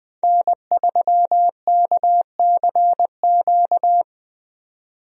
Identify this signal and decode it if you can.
Morse code, heard as a keyed tone, about 20 wpm, 700 Hz. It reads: N3KCQ
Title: USA Call Signs - 5 Characters - 20wpm